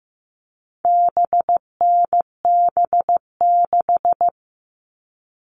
Morse code, heard as a keyed tone, about 15 wpm, 700 Hz. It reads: BNB6
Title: All Cut Numbers - 4-Digit 15wpm